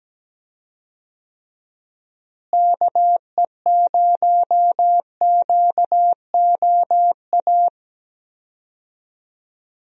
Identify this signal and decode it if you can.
Morse code, heard as a keyed tone, about 17 wpm, 700 Hz. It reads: KE0QOA
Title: USA Call Signs - 6 Characters - Repeated 3x - 17wpm